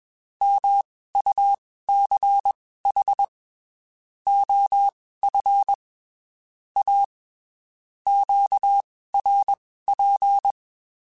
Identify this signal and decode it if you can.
Morse code, heard as a keyed tone, about 21 wpm, spaced as for 15 wpm, 800 Hz. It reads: MUCH OF A QRP
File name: QSOs / QSOs-mp3s / WD0BGP 1@20